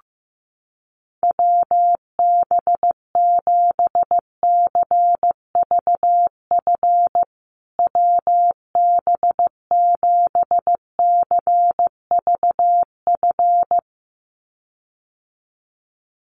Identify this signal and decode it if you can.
Morse code, heard as a keyed tone, about 15 wpm, 700 Hz. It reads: WB7CVF WB7CVF
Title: USA Call Signs Repeated 2x 15wpm